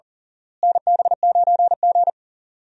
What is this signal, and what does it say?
Morse code, heard as a keyed tone, about 40 wpm, 700 Hz. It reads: NB9G
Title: USA Call Signs - 4 Characters - Repeated 3x - 40wpm